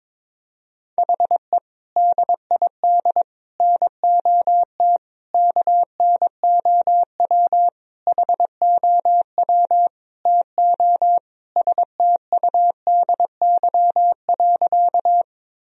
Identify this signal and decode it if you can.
Morse code, heard as a keyed tone, about 22 wpm, 700 Hz. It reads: HE DID NOT KNOW HOW TO STUDY.
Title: Sentences from Top 1000 Words - Review Part 1 - 22wpm